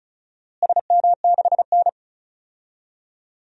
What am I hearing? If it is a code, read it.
Morse code, heard as a keyed tone, about 34 wpm, 700 Hz. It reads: SM6N
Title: DXCC-100 Call Signs - 4 Characters - Repeated 2x - 35wpm